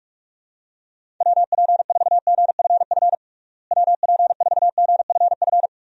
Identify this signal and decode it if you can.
Morse code, heard as a keyed tone, about 46 wpm, 700 Hz. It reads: WP4GFF WP4GFF
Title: USA Call Signs - 6 Characters - Repeated 3x - 45wpm